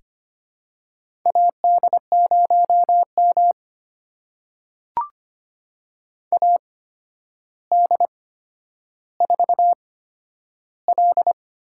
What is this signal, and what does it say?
Morse code, heard as a keyed tone, about 25 wpm, spaced as for 11 wpm, 700 Hz. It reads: AD0M AD4L
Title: USA Call Signs - 4 Characters - 8x Character Spacing - 25wpm